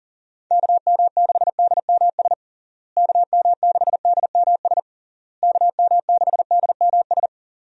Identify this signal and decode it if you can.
Morse code, heard as a keyed tone, about 37 wpm, 700 Hz. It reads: KM6DMS KM6DMS KM6DMS
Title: USA Call Signs - 6 Characters - Repeated 3x - 40wpm